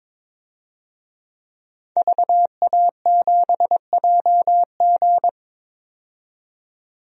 Morse code, heard as a keyed tone, about 22 wpm, 700 Hz. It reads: VA7JG
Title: DXCC-100 Call Signs - 5 Characters - 22wpm